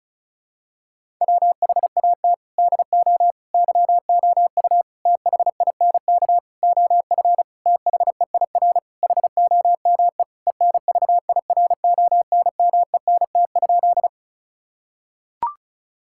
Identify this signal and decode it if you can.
Morse code, heard as a keyed tone, about 36 wpm, 700 Hz. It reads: WHAT DO YOU THINK OF THEIR HOME ENVIRONMENT?
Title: Sentences from Top 1000 Words - Review Part 5 - 35wpm